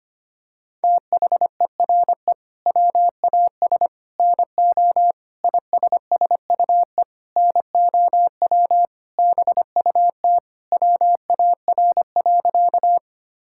Morse code, heard as a keyed tone, about 25 wpm, 700 Hz. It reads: THERE WAS NO ISSUE NOW BUT WAR.
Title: Sentences from Top 1000 Words - Review Part 6 - 25wpm